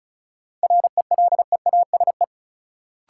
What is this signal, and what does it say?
Morse code, heard as a keyed tone, about 34 wpm, 700 Hz. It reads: RELEASE
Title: Sentences from Top 1100 Words - New Word Drill - 35wpm